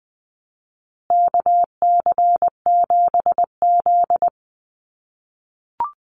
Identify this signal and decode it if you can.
Morse code, heard as a keyed tone, about 20 wpm, 700 Hz. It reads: KC7Z
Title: USA Call Signs Repeated 3x 20wpm